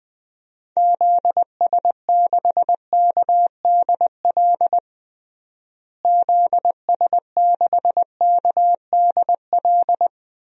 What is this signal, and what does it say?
Morse code, heard as a keyed tone, about 20 wpm, 700 Hz. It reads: ZS6KDL ZS6KDL
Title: DXCC-100 Call Signs - 6 Characters - Repeated 2x - 20wpm